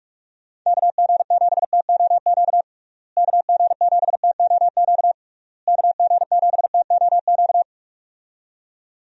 Morse code, heard as a keyed tone, about 46 wpm, 700 Hz. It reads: KG7TOQ KG7TOQ KG7TOQ
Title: USA Call Signs - 6 Characters - Repeated 3x - 45wpm